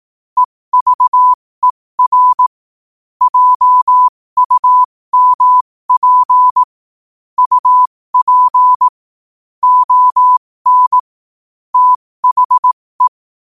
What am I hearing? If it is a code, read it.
Morse code, heard as a keyed tone, about 18 wpm, spaced as for 15 wpm, 1000 Hz. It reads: EVER JUMP UP ON THE
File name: qotdfiles / 15wpm